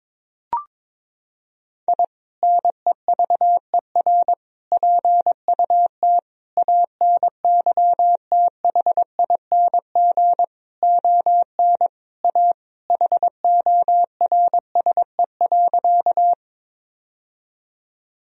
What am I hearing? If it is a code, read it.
Morse code, heard as a keyed tone, about 21 wpm, 700 Hz. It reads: I NEVER PUT ANYTHING ON A HORSE.
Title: Sentences from Top 1000 Words - Review Part 1 - 22wpm